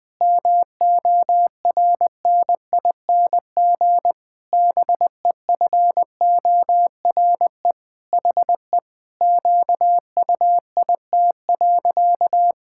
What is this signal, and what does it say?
Morse code, heard as a keyed tone, about 21 wpm, 700 Hz. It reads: MORNING BEFORE HE QUIT.
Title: Pangram Challenge v2 - 20wpm